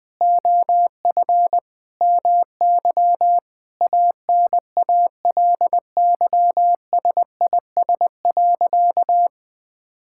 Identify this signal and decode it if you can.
Morse code, heard as a keyed tone, about 20 wpm, 700 Hz. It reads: OF MY ANALYSIS.
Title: Sentences from Top 1000 Words - Review Part 5 - 20wpm